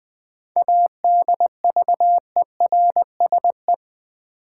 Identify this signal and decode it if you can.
Morse code, heard as a keyed tone, about 20 wpm, 700 Hz. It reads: ADVERSE
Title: 7 Letter Words - First 3 Letters Given - 20wpm